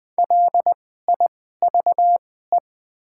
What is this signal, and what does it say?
Morse code, heard as a keyed tone, about 20 wpm, spaced as for 15 wpm, 700 Hz. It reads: LIVE
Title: 2x Character Spacing - Top 300 Words - 20wpm